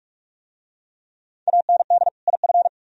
Morse code, heard as a keyed tone, about 44 wpm, 700 Hz. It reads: AND IF